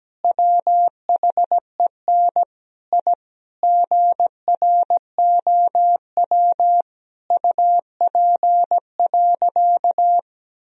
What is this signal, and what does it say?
Morse code, heard as a keyed tone, about 17 wpm, 700 Hz. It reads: WHEN I GROW UP.
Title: Sentences from Top 1000 Words - Review Part 1 - 17wpm